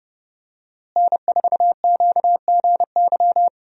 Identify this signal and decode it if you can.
Morse code, heard as a keyed tone, about 28 wpm, 700 Hz. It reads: N4QGY
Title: USA Call Signs - 5 Characters - Repeated 2x - 30wpm